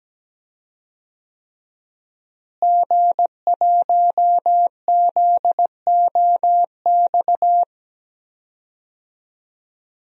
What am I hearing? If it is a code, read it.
Morse code, heard as a keyed tone, about 17 wpm, 700 Hz. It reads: G1ZOX
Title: DXCC-100 Call Signs - 5 Characters - Repeated 3x - 17wpm